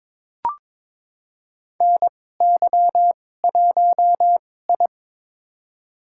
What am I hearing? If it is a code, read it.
Morse code, heard as a keyed tone, about 22 wpm, spaced as for 16 wpm, 700 Hz. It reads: NY1I
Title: USA Call Signs - 2x Character Spacing - 22wpm